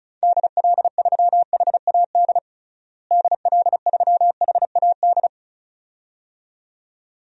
Morse code, heard as a keyed tone, about 36 wpm, 700 Hz. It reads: DL3HAD DL3HAD
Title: DXCC-100 Call Signs - 6 Characters - Repeated 2x - 35wpm